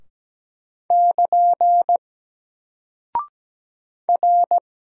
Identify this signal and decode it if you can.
Morse code, heard as a keyed tone, about 17 wpm, 700 Hz. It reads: <KN> R
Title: Single Character (Letter:Number:Punctuation:Prosign) Alt 17wpm